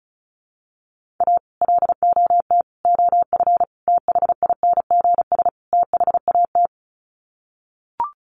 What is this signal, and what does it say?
Morse code, heard as a keyed tone, about 34 wpm, 700 Hz. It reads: A LOT OF THINGS THAT
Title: Sets of 5 Words 35wpm